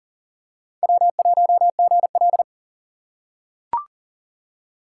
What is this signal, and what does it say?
Morse code, heard as a keyed tone, about 38 wpm, 700 Hz. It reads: W1GL
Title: USA Call Signs - 4 Characters - 6x Character Spacing - 40wpm